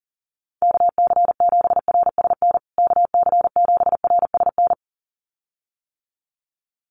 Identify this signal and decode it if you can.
Morse code, heard as a keyed tone, about 40 wpm, 700 Hz. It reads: KC7RSN KC7RSN
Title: USA Call Signs Repeated 2x 40wpm